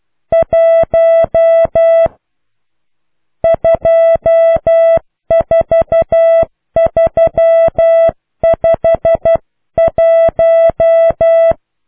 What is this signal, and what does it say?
Morse code, heard as a keyed tone, about 12 wpm, 660 Hz. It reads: 1 24351